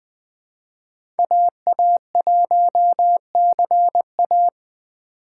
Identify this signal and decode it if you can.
Morse code, heard as a keyed tone, about 20 wpm, 700 Hz. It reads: AA1CA
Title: Speed Racing - USA 2x2 Call Signs 17wpm